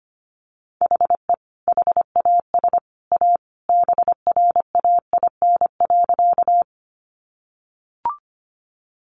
Morse code, heard as a keyed tone, about 25 wpm, 700 Hz. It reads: HE HAS A BRAIN.
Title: Sentences from Top 1100 Words 25wpm